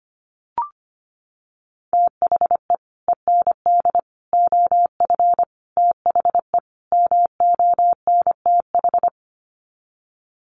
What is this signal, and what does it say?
Morse code, heard as a keyed tone, about 25 wpm, 700 Hz. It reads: THE END OF THE MONTH
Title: Sets of 5 Words Encore 25wpm